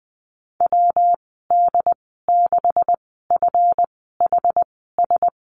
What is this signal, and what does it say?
Morse code, heard as a keyed tone, about 20 wpm, spaced as for 14 wpm, 700 Hz. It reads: WD6FHS
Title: 213 Lesson 34g - Number 8 - Call Signs 20wpm